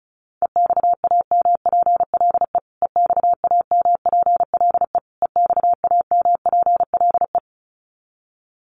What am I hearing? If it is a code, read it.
Morse code, heard as a keyed tone, about 34 wpm, 700 Hz. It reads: EXAMPLE EXAMPLE EXAMPLE
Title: Top 500 Words Repeated 35wpm